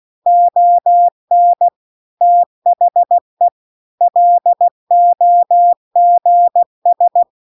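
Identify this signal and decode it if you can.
Morse code, heard as a keyed tone, about 16 wpm, 700 Hz. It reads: ON THE LOGS
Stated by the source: Morse practice text at 16 WPM.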